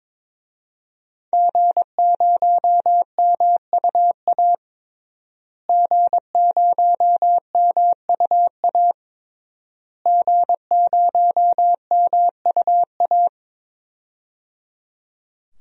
Morse code, heard as a keyed tone, about 22 wpm, 700 Hz. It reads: G0MUA G0MUA G0MUA
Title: DXCC-100 Call Signs - 5 Characters - Repeated 3x - 22wpm